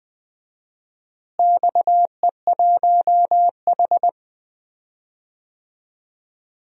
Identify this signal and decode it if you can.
Morse code, heard as a keyed tone, about 20 wpm, 700 Hz. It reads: XE1H
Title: DXCC-100 Call Signs - 4 Characters - Repeated 3x - 20wpm